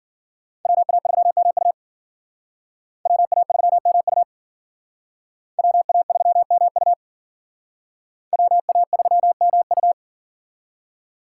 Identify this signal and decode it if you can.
Morse code, heard as a keyed tone, about 52 wpm, 700 Hz. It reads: WA3MU WA3MU WA3MU WA3MU